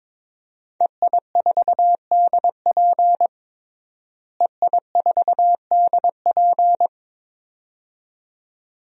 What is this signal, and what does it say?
Morse code, heard as a keyed tone, about 22 wpm, 700 Hz. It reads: EI4DP EI4DP
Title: DXCC-100 Call Signs - 5 Characters - Repeated 2x - 22wpm